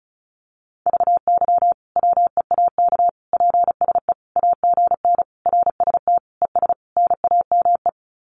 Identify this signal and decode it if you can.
Morse code, heard as a keyed tone, about 34 wpm, 700 Hz. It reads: VY WEAK PSE AGN RST ES NAME
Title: Common QSO Phrases 35wpm